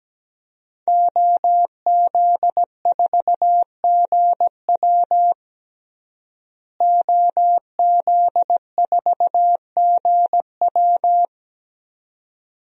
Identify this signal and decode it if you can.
Morse code, heard as a keyed tone, about 17 wpm, 700 Hz. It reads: OZ4GW OZ4GW
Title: DXCC-100 Call Signs - 5 Characters - Repeated 2x - 17wpm